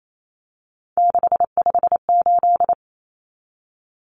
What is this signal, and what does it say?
Morse code, heard as a keyed tone, about 28 wpm, 700 Hz. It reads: 658
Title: Numbers - 3-Digit 28wpm